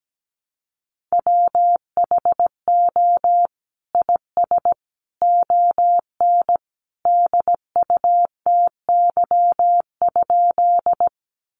Morse code, heard as a keyed tone, about 17 wpm, 700 Hz. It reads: WHO IS ON DUTY?
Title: Sentences from Top 1500 Words 17wpm